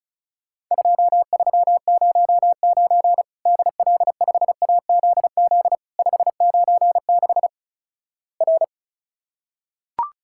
Morse code, heard as a keyed tone, about 36 wpm, 700 Hz. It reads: 2309 DL5AZZ 596 R
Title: Worked All Europe DX Contest - QTC Exchange - 35wpm